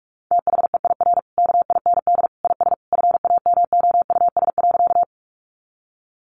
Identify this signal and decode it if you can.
Morse code, heard as a keyed tone, about 44 wpm, 700 Hz. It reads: THEIR KIND IS FAMOUS.
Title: Sentences from Top 1500 Words 45wpm